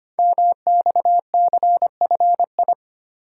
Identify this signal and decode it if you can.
Morse code, heard as a keyed tone, about 25 wpm, 700 Hz. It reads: MXCFI
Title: Code Groups - 5 Letters - 25wpm